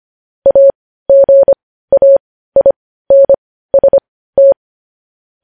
Morse code, heard as a keyed tone, about 25 wpm, spaced as for 15 wpm, 550 Hz. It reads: AGAINST
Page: news headlines in morse code at 15 wpm